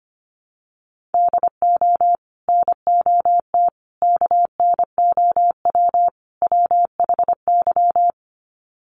Morse code, heard as a keyed tone, about 25 wpm, 700 Hz. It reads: DO NOT KNOW WHY
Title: Sets of 4 Words 25wpm